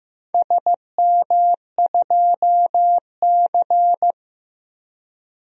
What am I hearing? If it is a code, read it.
Morse code, heard as a keyed tone, about 15 wpm, 700 Hz. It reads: SM2C
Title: DXCC-100 Call Signs - 4 Characters - Repeated 3x - 15wpm